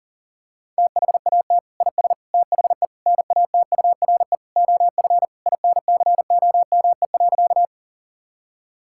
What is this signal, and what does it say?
Morse code, heard as a keyed tone, about 40 wpm, 700 Hz. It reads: THAT IS THE NATURE OF INCOME.
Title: Sentences from Top 1000 Words - Review Part 5 - 40wpm